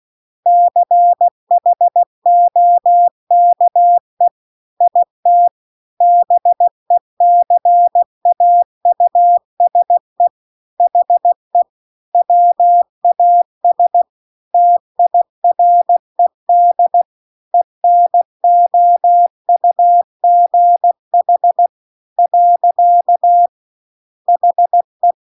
Morse code, heard as a keyed tone, about 16 wpm, 700 Hz. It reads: CHOKE IT BECAUSE HE WAS TIRED ENOUGH . HE